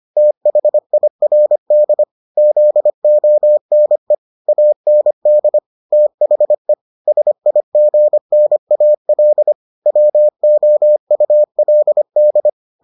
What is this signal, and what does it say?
Morse code, heard as a keyed tone, about 25 wpm, 600 Hz. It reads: THIRD ZONE AND THE SIGNAL WOULD